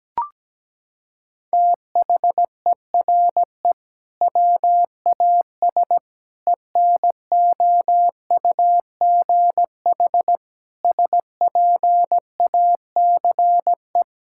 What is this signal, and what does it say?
Morse code, heard as a keyed tone, about 17 wpm, 700 Hz. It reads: THERE WAS ENOUGH SPACE
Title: Sentences from Top 1000 Words - Review Part 6 - 17wpm